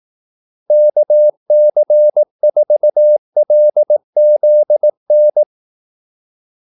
Morse code, heard as a keyed tone, about 18 wpm, 600 Hz. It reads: KC4LZN
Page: Kaldesignaler 18 wpm